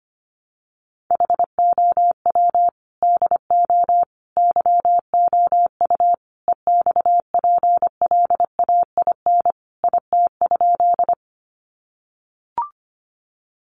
Morse code, heard as a keyed tone, about 25 wpm, 700 Hz. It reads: HOW DO YOU EXPLAIN IT?
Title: Sentences from Top 1100 Words 25wpm